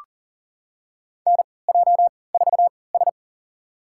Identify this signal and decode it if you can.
Morse code, heard as a keyed tone, about 40 wpm, spaced as for 23 wpm, 700 Hz. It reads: NJ4S
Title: USA Call Signs - 4 Characters - 3x Character Spacing - 40wpm